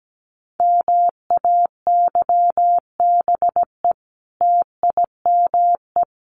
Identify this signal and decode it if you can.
Morse code, heard as a keyed tone, about 17 wpm, 700 Hz. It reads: MAYBE TIME
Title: A Sound Of Thunder 17wpm